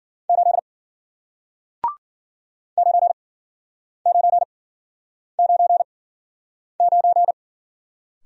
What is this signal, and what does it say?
Morse code, heard as a keyed tone, about 52 wpm, spaced as for 9 wpm, 700 Hz. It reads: 8 9999